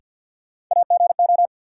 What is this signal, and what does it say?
Morse code, heard as a keyed tone, about 52 wpm, 700 Hz. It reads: AGO
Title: Sentences from Top 500 Words - New Word Drill - 50wpm